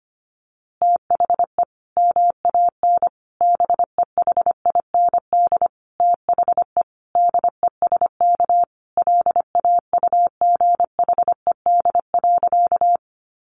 Morse code, heard as a keyed tone, about 25 wpm, 700 Hz. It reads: THE MAN BEHIND THE DESK LAUGHED.
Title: A Sound Of Thunder 25wpm